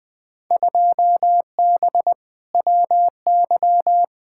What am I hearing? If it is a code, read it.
Morse code, heard as a keyed tone, about 20 wpm, 700 Hz. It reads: 2B WY
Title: Field Day Exchanges Encore - 20wpm